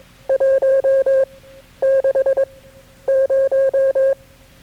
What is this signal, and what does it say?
Morse code, heard as a keyed tone, about 21 wpm, spaced as for 11 wpm, 540 Hz. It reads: 160
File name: KB6MT-tape3-side1